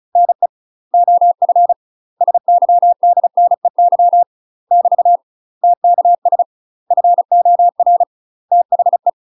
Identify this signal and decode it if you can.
Morse code, heard as a keyed tone, about 36 wpm, 700 Hz. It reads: NE OF SYDNEY <BT> TKS FOR THE
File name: UK MASTER QSOs TRAINING / 35WPM / UK MASTER QSOs 0020